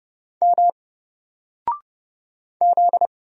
Morse code, heard as a keyed tone, about 30 wpm, 700 Hz. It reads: M Z
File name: Single-Letter-Number-30wpm